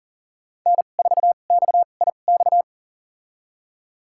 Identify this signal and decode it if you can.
Morse code, heard as a keyed tone, about 38 wpm, spaced as for 28 wpm, 700 Hz. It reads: N4XIX
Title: USA Call Signs - 5 Characters - 2x Character Spacing - 40wpm